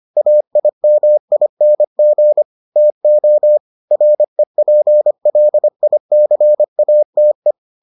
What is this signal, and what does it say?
Morse code, heard as a keyed tone, about 25 wpm, 600 Hz. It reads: AIMING TO REPLICATE